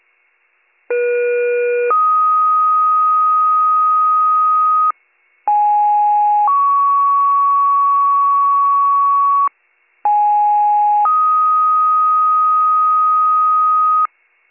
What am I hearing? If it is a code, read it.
Two-tone page, heard as a one-second tone, then a three-second tone, about 480 then 1250 Hz, about 830 then 1120 Hz, about 830 then 1250 Hz. It is heard from 0.9 to 14.1 s.